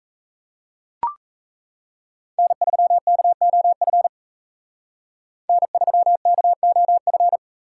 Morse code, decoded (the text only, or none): N3KOF N3KOF